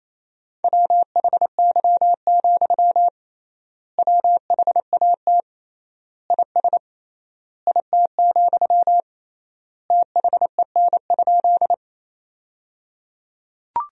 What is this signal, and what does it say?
Morse code, heard as a keyed tone, about 27 wpm, 700 Hz. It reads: WHY, WHAT IS IT, THEN?
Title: 3x Word-Spacing - Sentences from Top 200 Words - 28wpm